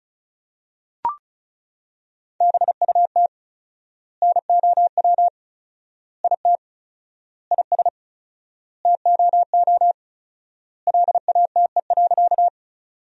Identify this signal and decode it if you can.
Morse code, heard as a keyed tone, about 36 wpm, 700 Hz. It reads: BUT NOW IT IS TOO LATE.
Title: 4x Word-Spacing - Sentences from Top 300 Words - 35wpm